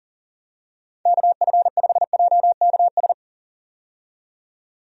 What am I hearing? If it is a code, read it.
Morse code, heard as a keyed tone, about 40 wpm, 700 Hz. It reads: KF5JKS
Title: USA Call Signs - 6 Characters - 40wpm